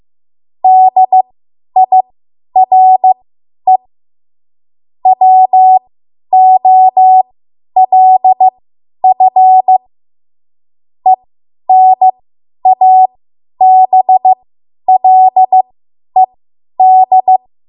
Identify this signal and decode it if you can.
Morse code, heard as a keyed tone, about 15 wpm, spaced as for 10 wpm, 750 Hz. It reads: DIRE WOLF ENABLED
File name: Morse / 10WPM